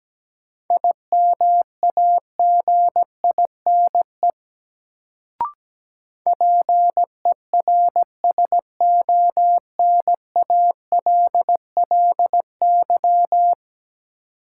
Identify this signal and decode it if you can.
Morse code, heard as a keyed tone, about 17 wpm, 700 Hz. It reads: IMAGINE PERSONALLY
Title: Sentences from Top 1400 Words - New Word Drill Alt - 17wpm